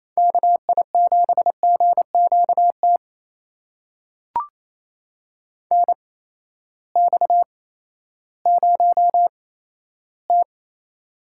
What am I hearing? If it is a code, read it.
Morse code, heard as a keyed tone, about 28 wpm, spaced as for 14 wpm, 700 Hz. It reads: KI7GQT NX0T
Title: USA Call Signs - 8x Character Spacing - 28wpm